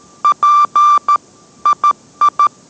Morse code, heard as a keyed tone, about 13 wpm, 1180 Hz. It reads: P II